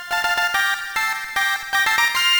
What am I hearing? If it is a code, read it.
There is no code.